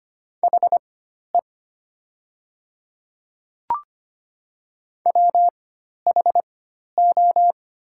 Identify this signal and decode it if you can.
Morse code, heard as a keyed tone, about 25 wpm, spaced as for 12 wpm, 700 Hz. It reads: HE WHO
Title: 4x Character Spacing - Top 100 Words - 25wpm